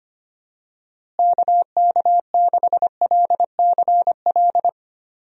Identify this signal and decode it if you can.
Morse code, heard as a keyed tone, about 25 wpm, 700 Hz. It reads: KK6LCL
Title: USA Call Signs - 6 Characters - Repeated 2x - 25wpm